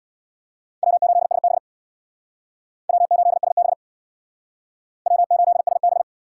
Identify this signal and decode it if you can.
Morse code, heard as a keyed tone, about 64 wpm, 700 Hz. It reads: W8SB W8SB W8SB